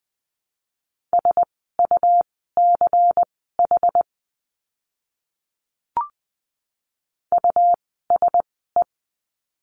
Morse code, heard as a keyed tone, about 20 wpm, spaced as for 15 wpm, 700 Hz. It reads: SUCH USE
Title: 237 Lesson 38f - Letter q - Words 20wpm